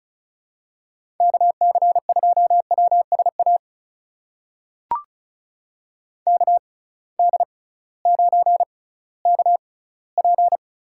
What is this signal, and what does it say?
Morse code, heard as a keyed tone, about 36 wpm, spaced as for 19 wpm, 690 Hz. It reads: KC2WSA KD9KP
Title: USA Call Signs - 6 Characters - 6x Character Spacing - 35wpm